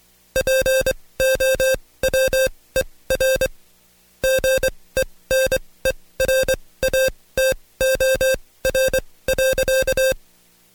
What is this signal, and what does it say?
Morse code, heard as a keyed tone, about 23 wpm, spaced as for 17 wpm, 530 Hz. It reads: POWER GENERATOR.
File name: Code Quick / AudioDownld2 / Seg135-Stepper8 17WPM